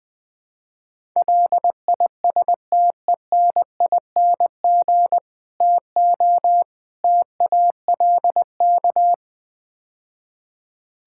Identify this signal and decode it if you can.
Morse code, heard as a keyed tone, about 21 wpm, 700 Hz. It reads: LISTENING TO TALK
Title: Build A Phrase - 5-Word Phrases - 20wpm